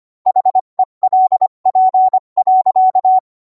Morse code, heard as a keyed tone, about 25 wpm, 750 Hz. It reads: HELP.